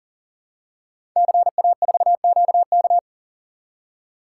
Morse code, heard as a keyed tone, about 38 wpm, 690 Hz. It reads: CA4QK
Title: Code Groups - 5 Letter:Numbers - 40wpm